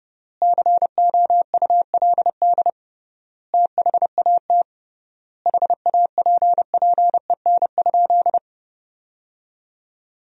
Morse code, heard as a keyed tone, about 30 wpm, 700 Hz. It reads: COULD THAT HAPPEN?
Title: 3x Word-Spacing - Sentences from Top 400 Words - 30wpm